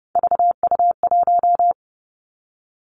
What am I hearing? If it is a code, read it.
Morse code, heard as a keyed tone, about 30 wpm, 700 Hz. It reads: VU1
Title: Three Challenging Letter-Number Combinations 30wpm